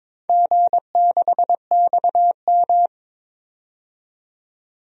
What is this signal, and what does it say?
Morse code, heard as a keyed tone, about 22 wpm, 700 Hz. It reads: G6XM
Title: DXCC-100 Call Signs - 4 Characters - 22wpm